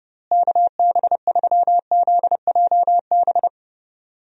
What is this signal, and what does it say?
Morse code, heard as a keyed tone, about 30 wpm, 700 Hz. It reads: KB3ZJB
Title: USA Call Signs - 6 Characters - Repeated 3x - 30wpm